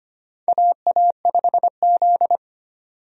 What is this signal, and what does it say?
Morse code, heard as a keyed tone, about 25 wpm, 700 Hz. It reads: AA5Z
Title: USA Call Signs - 4 Characters - 25wpm